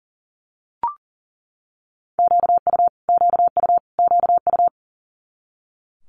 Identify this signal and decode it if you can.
Morse code, heard as a keyed tone, about 38 wpm, 700 Hz. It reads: QU QU QU
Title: Multi-letter Phonemes Repeated 3x 40wpm